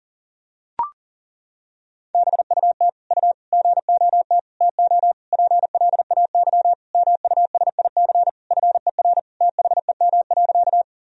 Morse code, heard as a keyed tone, about 40 wpm, 700 Hz. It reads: BUT U GOT TO PLAY MUSIC FER THEM.
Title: Sentences from Top 300 Words and Abbreviations 40wpm